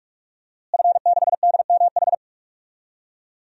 Morse code, heard as a keyed tone, about 43 wpm, 700 Hz. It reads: F6DMH
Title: DXCC-100 Call Signs - 5 Characters - Repeated 3x - 45wpm